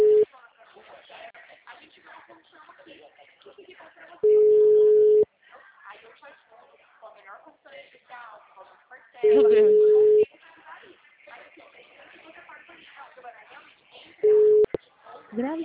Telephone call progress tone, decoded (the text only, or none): ringback tone